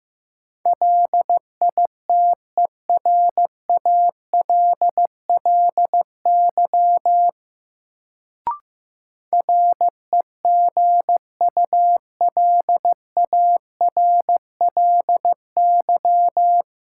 Morse code, heard as a keyed tone, about 15 wpm, 700 Hz. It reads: LITERALLY REGULARLY
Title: Sentences from Top 1400 Words - New Word Drill Alt - 15wpm